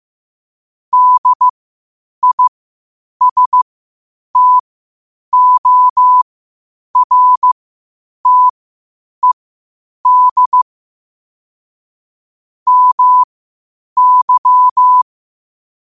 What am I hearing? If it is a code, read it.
Morse code, heard as a keyed tone, about 15 wpm, spaced as for 8 wpm, 1000 Hz. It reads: DISTORTED MY
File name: qotdfiles / 8wpm